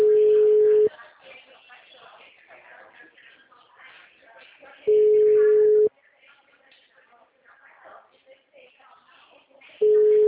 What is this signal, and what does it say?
Ringback tone (Europe, 425 Hz), with speech.
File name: out- -2007- - - .41368